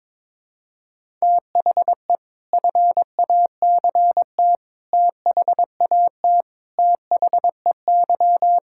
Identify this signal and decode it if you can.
Morse code, heard as a keyed tone, about 22 wpm, 700 Hz. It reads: THE FACT THAT THEY
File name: practice / 22wpm / Sentences with 4-Word Phrases - 22wpm